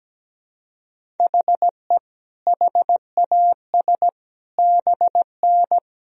Morse code, heard as a keyed tone, about 17 wpm, 700 Hz. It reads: HE HAS BN
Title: Sentences from Top 300 Words and Abbreviations 17wpm